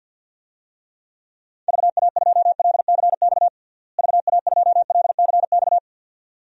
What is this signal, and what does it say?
Morse code, heard as a keyed tone, about 52 wpm, 700 Hz. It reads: VA2LCX VA2LCX
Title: DXCC-100 Call Signs - 6 Characters - Repeated 2x - 50wpm